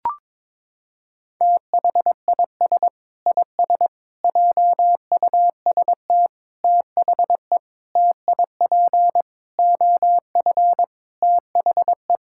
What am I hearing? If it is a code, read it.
Morse code, heard as a keyed tone, about 22 wpm, 700 Hz. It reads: THIS IS JUST THE TIP OF THE
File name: practice / 22wpm / Sets of 8 Words - 22wpm